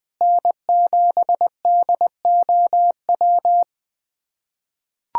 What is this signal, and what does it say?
Morse code, heard as a keyed tone, about 20 wpm, 700 Hz. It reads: N7DOW
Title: USA Callsign DE Callsign - 5 Characters - 20wpm